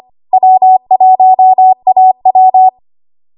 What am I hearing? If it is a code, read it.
Morse code, heard as a keyed tone, about 25 wpm, 750 Hz. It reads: W1AW
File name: Morse / 25WPM